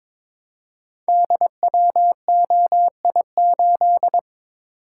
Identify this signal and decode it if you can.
Morse code, heard as a keyed tone, about 22 wpm, 700 Hz. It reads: DWOI8
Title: Code Groups - 5 Letter:Numbers - 22wpm